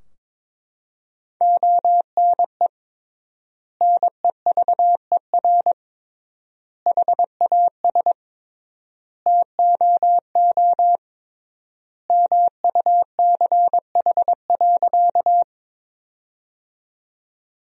Morse code, heard as a keyed tone, about 22 wpm, 700 Hz. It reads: ONE NEVER HAS TOO MUCH.
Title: 3x Word-Spacing - Sentences from Top 300 Words - 22wpm